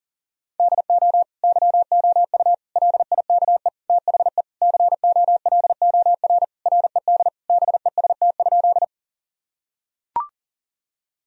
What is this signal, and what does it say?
Morse code, heard as a keyed tone, about 40 wpm, 700 Hz. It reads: DO YOU LIKE THE COLOR RED BEST?
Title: Sentences from Top 1000 Words - Review Part 2 - 40wpm